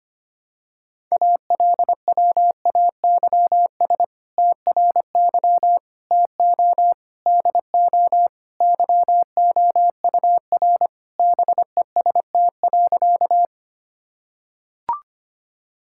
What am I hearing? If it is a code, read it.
Morse code, heard as a keyed tone, about 25 wpm, 700 Hz. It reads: ALWAYS TRY TO DO YOUR BEST.
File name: practice / 25wpm / Sentences with 2-Word Phrases - 25wpm